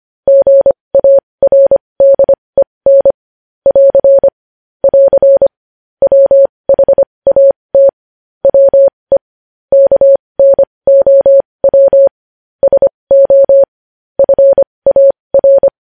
Morse code, heard as a keyed tone, about 25 wpm, spaced as for 20 wpm, 550 Hz. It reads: GARDEN <AR> <AR> WHAT WE KNOW SO FAR
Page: news-headlines-in-morse-code-at-20-wpm